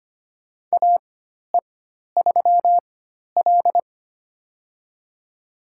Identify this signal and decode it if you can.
Morse code, heard as a keyed tone, about 25 wpm, spaced as for 12 wpm, 700 Hz. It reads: AE3L
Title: USA Call Signs - 4 Characters - 4x Character Spacing - 25wpm